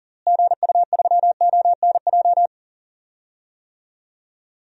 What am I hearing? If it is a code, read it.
Morse code, heard as a keyed tone, about 38 wpm, 700 Hz. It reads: GU3ONJ
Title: DXCC-100 Call Signs - 6 Characters - Repeated 2x - 40wpm